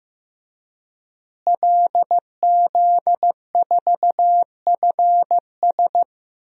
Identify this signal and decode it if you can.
Morse code, heard as a keyed tone, about 15 wpm, 700 Hz. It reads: LZ4FS
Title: DXCC-100 Call Signs - 5 Characters - Repeated 3x - 15wpm